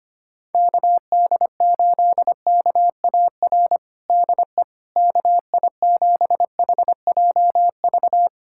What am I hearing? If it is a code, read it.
Morse code, heard as a keyed tone, about 25 wpm, 700 Hz. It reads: KD8KAR DE KI7HJV
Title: USA Callsign DE Callsign - 6 Characters - 25wpm